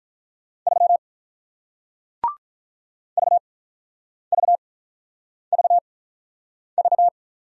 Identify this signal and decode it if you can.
Morse code, heard as a keyed tone, about 48 wpm, spaced as for 9 wpm, 700 Hz. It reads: 3 VVVV